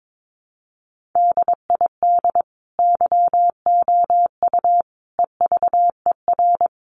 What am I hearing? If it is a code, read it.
Morse code, heard as a keyed tone, about 22 wpm, 700 Hz. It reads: DID YOU EVER
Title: Sentences from Top 1500 Words 22wpm